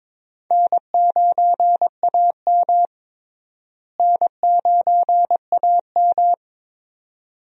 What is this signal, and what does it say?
Morse code, heard as a keyed tone, about 21 wpm, 700 Hz. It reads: N9AM N9AM
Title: USA Call Signs - 4 Characters - Repeated 2x - 22wpm